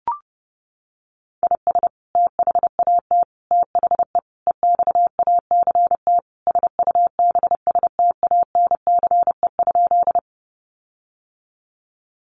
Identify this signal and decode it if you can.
Morse code, heard as a keyed tone, about 30 wpm, 700 Hz. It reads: IS THAT THE EXACT SUBSTANCE?
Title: Sentences from Top 1500 Words 30wpm